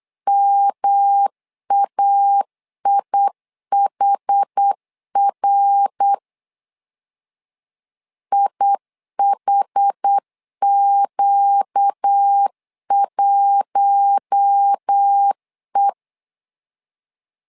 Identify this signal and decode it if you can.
Morse code, heard as a keyed tone, about 8 wpm, 800 Hz. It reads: MAIHR IHQ1E
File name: LEZIONI / 4 Q1 8wpm 3035